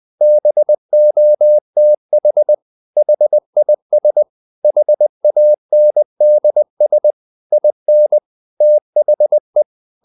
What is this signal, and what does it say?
Morse code, heard as a keyed tone, about 20 wpm, 600 Hz. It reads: BOTH HIS HANDS IN THE
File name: the old man and the sea part 2 20wpm